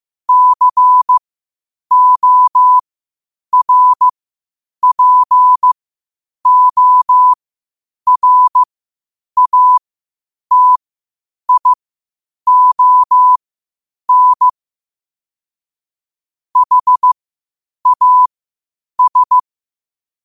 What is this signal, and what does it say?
Morse code, heard as a keyed tone, about 15 wpm, spaced as for 8 wpm, 1000 Hz. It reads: CORPORATION HAS
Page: Quotes for Fri, 15 Aug 2025 in Morse Code at 8 words per minute.